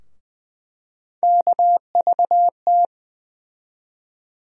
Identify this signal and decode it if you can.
Morse code, heard as a keyed tone, about 20 wpm, 700 Hz. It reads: KVT
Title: USA Call Sign Postfixes - 3 Letters 20wpm